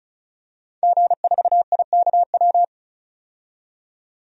Morse code, heard as a keyed tone, about 36 wpm, 700 Hz. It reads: G4IKW
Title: DXCC-100 Call Signs - 5 Characters - 35wpm